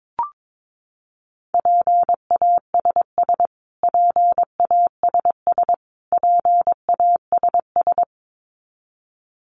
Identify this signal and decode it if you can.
Morse code, heard as a keyed tone, about 22 wpm, 700 Hz. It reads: PASS PASS PASS
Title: Top 500 Words Repeated 22wpm